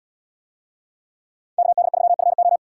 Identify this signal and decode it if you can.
Morse code, heard as a keyed tone, about 77 wpm, 700 Hz. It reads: KD3FO